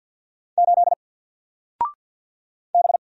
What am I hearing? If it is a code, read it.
Morse code, heard as a keyed tone, about 50 wpm, 700 Hz. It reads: 8 B
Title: Single Letter-Number Alt 50wpm